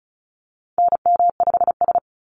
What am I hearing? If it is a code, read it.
Morse code, heard as a keyed tone, about 36 wpm, 700 Hz. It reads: NM5S
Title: NA SOTA Activator Callsigns 35wpm